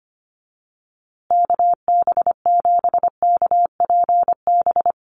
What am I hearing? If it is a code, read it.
Morse code, heard as a keyed tone, about 25 wpm, 700 Hz. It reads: KB7KPB
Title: USA Call Signs Repeated 3x 25wpm